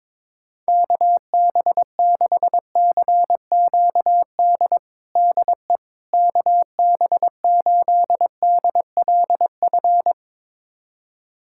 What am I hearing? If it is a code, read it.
Morse code, heard as a keyed tone, about 22 wpm, 700 Hz. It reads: KB6CQD DE KB8DLF
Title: USA Callsign DE Callsign 22wpm